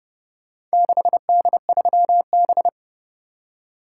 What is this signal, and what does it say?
Morse code, heard as a keyed tone, about 29 wpm, 700 Hz. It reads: 6D3B